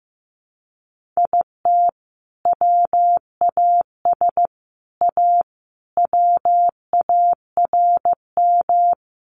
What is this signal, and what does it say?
Morse code, heard as a keyed tone, about 15 wpm, 700 Hz. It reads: IT WAS A WARM
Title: Sentences from Top 1500 Words 15wpm